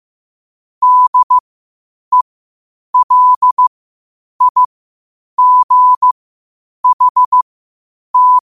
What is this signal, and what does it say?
Morse code, heard as a keyed tone, about 15 wpm, spaced as for 8 wpm, 1000 Hz. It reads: DELIGHT
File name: qotdfiles / 8wpm